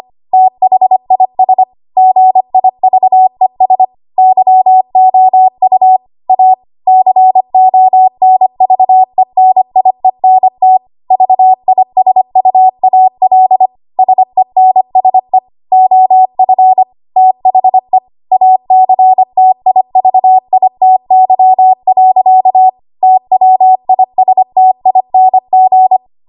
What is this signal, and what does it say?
Morse code, heard as a keyed tone, about 25 wpm, 750 Hz. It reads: THIS GIVES YOU A CONVENIENT VISUAL SENSE OF THE ACTIVITY. TWISTING
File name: Archive / 25 WPM / 25WPM